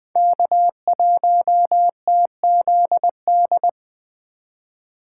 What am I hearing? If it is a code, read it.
Morse code, heard as a keyed tone, about 24 wpm, 700 Hz. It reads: K1TZD
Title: Speed Racing - USA Call Signs 20wpm